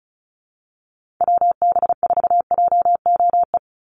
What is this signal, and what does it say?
Morse code, heard as a keyed tone, about 34 wpm, 700 Hz. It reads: WB4JOE
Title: USA Call Signs 35wpm